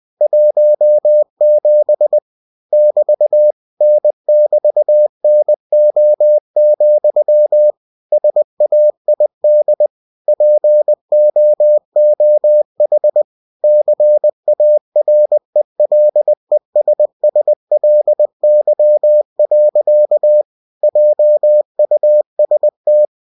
Morse code, heard as a keyed tone, about 20 wpm, 600 Hz. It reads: 17 <BT> N<BT>NO, SAID POOH CARELESSLY. JUST
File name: cw-books / winnie-the-pooh / winnie-20wpm-page-0017